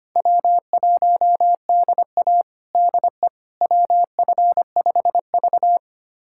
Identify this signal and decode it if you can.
Morse code, heard as a keyed tone, about 25 wpm, 700 Hz. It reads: W1DA DE WF5V
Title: USA Callsign DE Callsign - 4 Characters - 25wpm